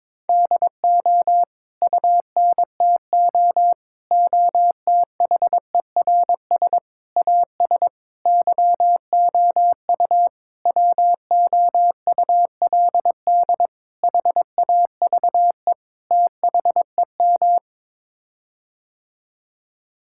Morse code, heard as a keyed tone, about 22 wpm, 700 Hz. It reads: DO UNTO OTHERS AS YOU WOULD HAVE THEM
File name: practice / 22wpm / Sets of 8 Words - 22wpm